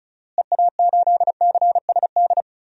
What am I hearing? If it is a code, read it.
Morse code, heard as a keyed tone, about 34 wpm, 700 Hz. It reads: EA8CSD
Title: DXCC-100 Call Signs - 6 Characters - Repeated 3x - 35wpm